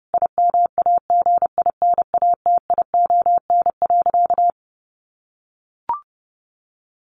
Morse code, heard as a keyed tone, about 30 wpm, 700 Hz. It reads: IMAGINATION.
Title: Sentences from Top 1400 Words 30wpm